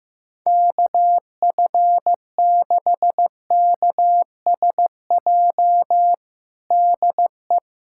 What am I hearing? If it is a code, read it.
Morse code, heard as a keyed tone, about 15 wpm, 700 Hz. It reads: KF6KSJ DE
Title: USA Callsign DE Callsign 15wpm